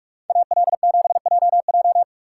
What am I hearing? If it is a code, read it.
Morse code, heard as a keyed tone, about 44 wpm, 700 Hz. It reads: AL7JJ